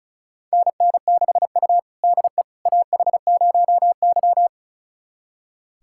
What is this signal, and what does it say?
Morse code, heard as a keyed tone, about 36 wpm, 700 Hz. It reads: NN6U DE AH0Y
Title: USA Callsign DE Callsign - 4 Characters - 35wpm